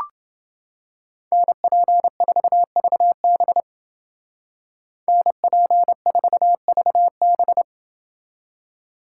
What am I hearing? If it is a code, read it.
Morse code, heard as a keyed tone, about 28 wpm, 700 Hz. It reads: NP4VB NP4VB